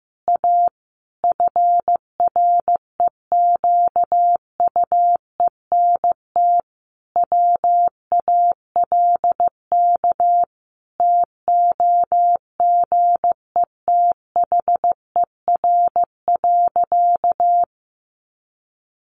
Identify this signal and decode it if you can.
Morse code, heard as a keyed tone, about 15 wpm, 700 Hz. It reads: A FREQUENT WALK TOGETHER.
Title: Sentences from Top 1400 Words 15wpm